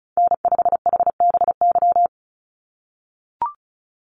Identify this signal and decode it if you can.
Morse code, heard as a keyed tone, about 34 wpm, 700 Hz. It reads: N5HBY
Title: USA Call Signs 35wpm